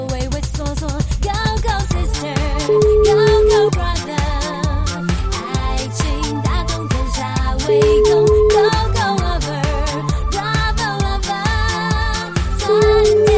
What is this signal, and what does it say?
Ringback tone (Europe, 425 Hz), with noise.